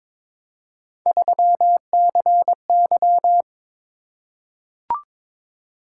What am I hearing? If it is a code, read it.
Morse code, heard as a keyed tone, about 22 wpm, 700 Hz. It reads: 3CY
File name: practice / 22wpm / Three Letter-Number Combinations - 22wpm